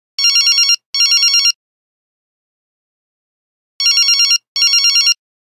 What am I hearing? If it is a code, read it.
There is no code.